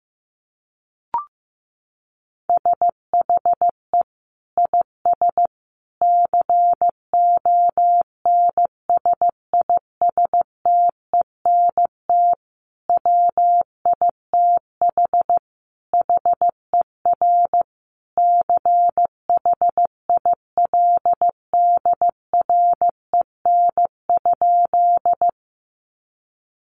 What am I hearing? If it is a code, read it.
Morse code, heard as a keyed tone, about 15 wpm, 700 Hz. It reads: SHE IS CONSISTENT WITH HER CHILDREN?
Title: Sentences from Top 1500 Words 15wpm